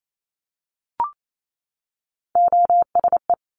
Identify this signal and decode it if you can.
Morse code, heard as a keyed tone, about 28 wpm, 700 Hz. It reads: OSE
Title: Top Three Letter Combinations in Words 28wpm